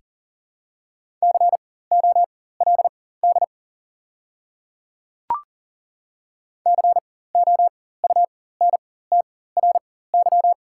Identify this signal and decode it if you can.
Morse code, heard as a keyed tone, about 40 wpm, spaced as for 19 wpm, 700 Hz. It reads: COLD COUNTRY
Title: 4x Character Spacing - Top 500 Words - 40wpm